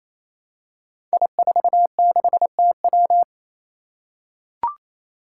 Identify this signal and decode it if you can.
Morse code, heard as a keyed tone, about 27 wpm, 700 Hz. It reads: I46TW
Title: Five Challenging Letter-Number Combinations 28wpm